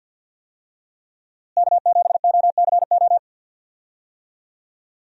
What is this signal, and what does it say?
Morse code, heard as a keyed tone, about 52 wpm, 700 Hz. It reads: K7OCO
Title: USA Call Signs - 5 Characters - Repeated 3x - 50wpm